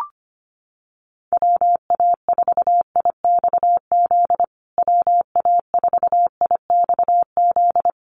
Morse code, heard as a keyed tone, about 25 wpm, 700 Hz. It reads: WA4IXZ WA4IXZ
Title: USA Call Signs Repeated 3x 25wpm